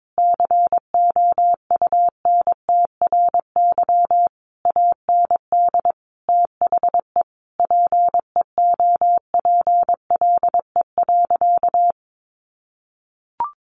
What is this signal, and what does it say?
Morse code, heard as a keyed tone, about 22 wpm, 700 Hz. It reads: COUNTRY AND THE PEOPLE.
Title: Sentences from Top 1500 Words 22wpm